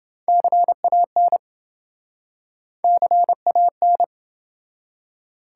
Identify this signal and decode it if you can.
Morse code, heard as a keyed tone, about 28 wpm, 700 Hz. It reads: CAN CAN